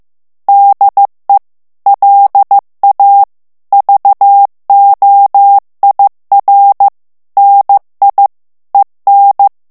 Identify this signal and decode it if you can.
Morse code, heard as a keyed tone, about 15 wpm, 800 Hz. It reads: DE LA VOIR NI EN
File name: morse mp3 / FRANCAIS TEXTE CLAIR 2 VITESSE 14 MOTS MINUTES